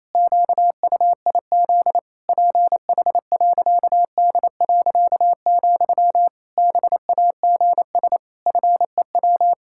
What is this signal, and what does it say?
Morse code, heard as a keyed tone, about 28 wpm, 700 Hz. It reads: QUIZ PH.D., BAGS FEW
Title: Pangram Challenge v2 - 28wpm